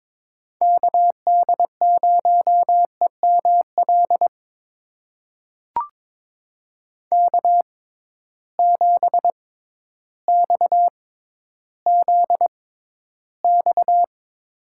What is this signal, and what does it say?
Morse code, heard as a keyed tone, about 22 wpm, spaced as for 12 wpm, 700 Hz. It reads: KD0EML K7XZX
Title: USA Call Signs - 6x Character Spacing - 22wpm